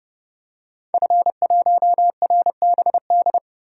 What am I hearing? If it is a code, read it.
Morse code, heard as a keyed tone, about 30 wpm, 700 Hz. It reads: F1RBD
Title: DXCC-100 Call Signs - 5 Characters - Repeated 2x - 30wpm